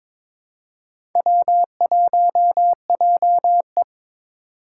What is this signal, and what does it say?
Morse code, heard as a keyed tone, about 26 wpm, 700 Hz. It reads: W1JE
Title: Speed Racing - USA 1x2 Call Signs 22wpm